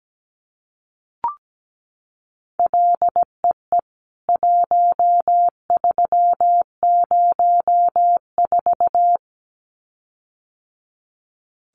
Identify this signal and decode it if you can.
Morse code, heard as a keyed tone, about 17 wpm, 700 Hz. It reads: LEE 1304
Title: CWT Exchange 17wpm